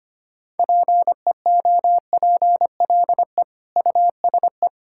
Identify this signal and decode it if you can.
Morse code, heard as a keyed tone, about 25 wpm, 700 Hz. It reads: PEOPLE USE
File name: practice / 25wpm / Difficult Sentences with Hints - 25wpm